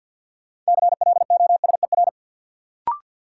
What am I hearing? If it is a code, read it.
Morse code, heard as a keyed tone, about 52 wpm, 700 Hz. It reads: CLOSER
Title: Sets of 5 Words Encore - New Word Drill - 50wpm